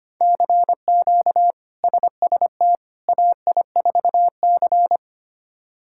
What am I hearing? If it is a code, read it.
Morse code, heard as a keyed tone, about 25 wpm, 700 Hz. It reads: CQ SST AI4C
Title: K1USN SST QSO Practice - 25wpm